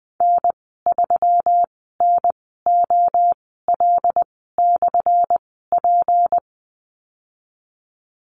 Morse code, heard as a keyed tone, about 19 wpm, spaced as for 14 wpm, 700 Hz. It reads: N3NOL/P
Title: 179 Lesson 29g - Slash - Call Signs 20wpm